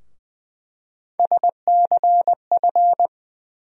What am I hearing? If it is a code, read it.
Morse code, heard as a keyed tone, about 20 wpm, 700 Hz. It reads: SCF
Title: USA Call Sign Postfixes - 3 Letters 20wpm